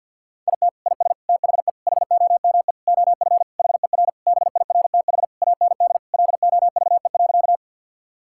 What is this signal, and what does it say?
Morse code, heard as a keyed tone, about 52 wpm, 700 Hz. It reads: IT IS THE HOME OF HER BIRTH AND LOVE.
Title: Sentences from Top 1000 Words - Review Part 5 - 50wpm